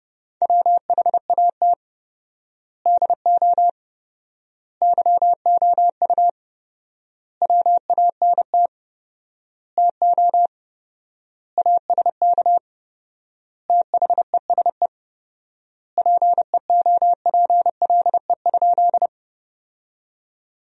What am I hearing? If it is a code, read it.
Morse code, heard as a keyed tone, about 30 wpm, 700 Hz. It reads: WHAT DO YOU WANT TO ASK THESE PEOPLE?
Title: 4x Word-Spacing - Sentences from Top 200 Words - 30wpm